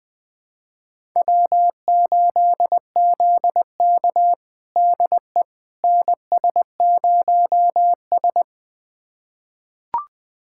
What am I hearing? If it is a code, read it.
Morse code, heard as a keyed tone, about 20 wpm, 700 Hz. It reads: W8ZK DE NS0S
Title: USA Callsign DE Callsign - 4 Characters - 20wpm